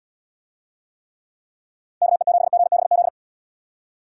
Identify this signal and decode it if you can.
Morse code, heard as a keyed tone, about 71 wpm, 700 Hz. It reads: KE7GBZ